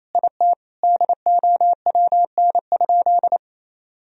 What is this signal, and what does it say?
Morse code, heard as a keyed tone, about 27 wpm, 700 Hz. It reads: IT DOWN?
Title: Sentences from Top 1000 Words - Review Part 1 - 28wpm